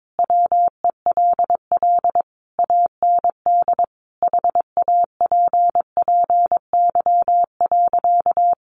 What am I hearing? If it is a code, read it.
Morse code, heard as a keyed tone, about 22 wpm, 700 Hz. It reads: WELL AND HAPPY.
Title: Sentences from Top 1000 Words 22wpm